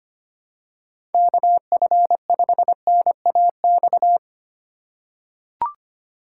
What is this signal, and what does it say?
Morse code, heard as a keyed tone, about 25 wpm, 700 Hz. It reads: KF5NAX
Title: USA Call Signs - 6 Characters - 3x Character Spacing - 25wpm